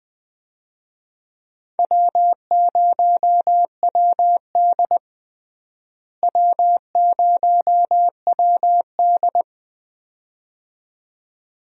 Morse code, heard as a keyed tone, about 20 wpm, 700 Hz. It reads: W0WD W0WD
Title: USA Call Signs - 4 Characters - Repeated 2x - 20wpm